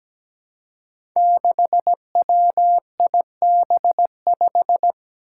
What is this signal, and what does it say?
Morse code, heard as a keyed tone, about 17 wpm, 700 Hz. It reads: 6WIB5
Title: Five Challenging Letter-Number Combinations 17wpm